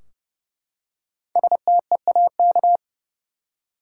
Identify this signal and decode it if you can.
Morse code, heard as a keyed tone, about 31 wpm, 700 Hz. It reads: STEAK
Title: Five Letter Words - 30wpm